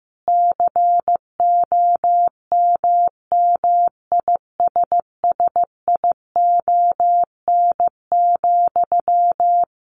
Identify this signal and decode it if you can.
Morse code, heard as a keyed tone, about 15 wpm, 700 Hz. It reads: COMMISSION,
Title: Sentences from Top 1400 Words 15wpm